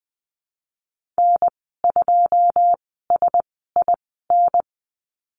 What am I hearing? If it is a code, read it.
Morse code, heard as a keyed tone, about 20 wpm, spaced as for 15 wpm, 700 Hz. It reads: N2SIN
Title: 088 Lesson 15g - Number 5 - Call Signs 20wpm